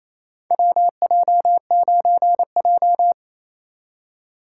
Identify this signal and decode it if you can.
Morse code, heard as a keyed tone, about 27 wpm, 700 Hz. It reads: WJ9J
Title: USA Call Signs - 4 Characters - 2x Character Spacing - 28wpm